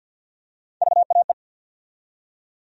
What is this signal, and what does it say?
Morse code, heard as a keyed tone, about 47 wpm, 700 Hz. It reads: VAE
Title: USA Call Sign Postfixes - 3 Letters 50wpm